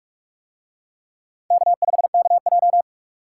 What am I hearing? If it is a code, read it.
Morse code, heard as a keyed tone, about 44 wpm, 700 Hz. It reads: K5KJ